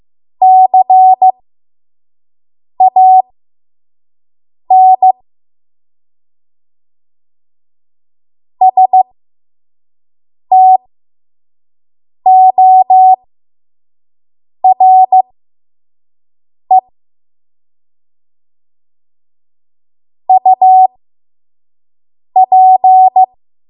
Morse code, heard as a keyed tone, about 15 wpm, spaced as for 5 wpm, 750 Hz. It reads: CAN STORE UP
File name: Morse / 05WPM